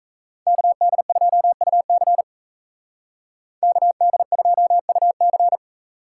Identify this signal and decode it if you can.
Morse code, heard as a keyed tone, about 35 wpm, 700 Hz. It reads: KD2UC KD2UC
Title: Speed Racing - USA Call Signs 28wpm